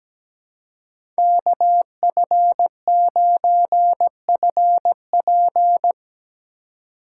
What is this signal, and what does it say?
Morse code, heard as a keyed tone, about 17 wpm, 700 Hz. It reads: KF9FP
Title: USA Call Signs - 5 Characters - 4x Character Spacing - 17wpm